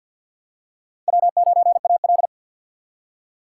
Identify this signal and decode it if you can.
Morse code, heard as a keyed tone, about 52 wpm, 700 Hz. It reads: W9AL
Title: USA Call Signs - 3x Character Spacing - 50wpm